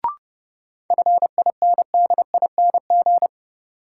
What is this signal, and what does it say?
Morse code, heard as a keyed tone, about 30 wpm, 700 Hz. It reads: FINDING
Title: Sentences from Top 1100 Words - New Word Drill - 30wpm